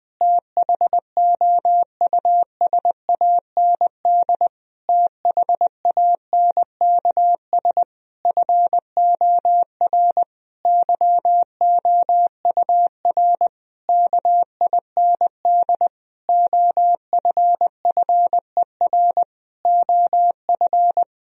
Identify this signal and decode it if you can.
Morse code, heard as a keyed tone, about 20 wpm, 700 Hz. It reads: THOUSAND THANKS FOR YOUR KIND OFFER OF
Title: Sentences from Top 1000 Words - Review Part 4 - 20wpm